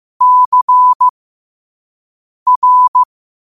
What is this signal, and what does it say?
Morse code, heard as a keyed tone, about 15 wpm, spaced as for 5 wpm, 1000 Hz. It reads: CR
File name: qotdfiles / 5wpm